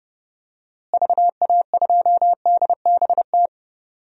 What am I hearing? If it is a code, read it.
Morse code, heard as a keyed tone, about 30 wpm, 700 Hz. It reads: VA2DBT
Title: DXCC-100 Call Signs - 6 Characters - Repeated 2x - 30wpm